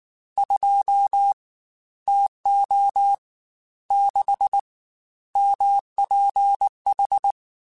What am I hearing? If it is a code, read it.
Morse code, heard as a keyed tone, about 19 wpm, 800 Hz. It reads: 2 TO 6 MPH
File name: Weather Sets / Weather Sets-mp3s / Weather 4 10 @18